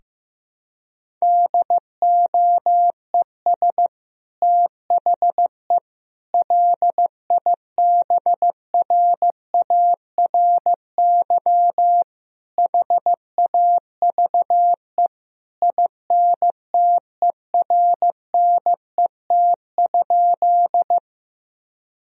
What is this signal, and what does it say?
Morse code, heard as a keyed tone, about 16 wpm, 700 Hz. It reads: DOES THE LIBRARY HAVE INTERNET?
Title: Sentences from Top 1000 Words - Review Part 4 - 15wpm